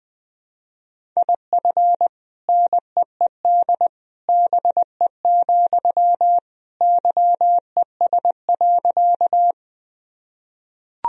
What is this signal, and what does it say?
Morse code, heard as a keyed tone, about 20 wpm, 700 Hz. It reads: IF NEED BE, YES.
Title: Sentences from Top 1000 Words - Review Part 1 - 20wpm